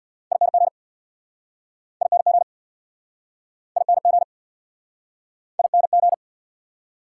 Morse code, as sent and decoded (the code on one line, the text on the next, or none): .. -. --. / .. -. --. / .. -. --. / .. -. --.
ING ING ING ING